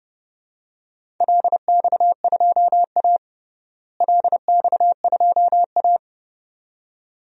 Morse code, heard as a keyed tone, about 30 wpm, 700 Hz. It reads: LX2A LX2A
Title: DXCC-100 Call Signs - 4 Characters - Repeated 2x - 30wpm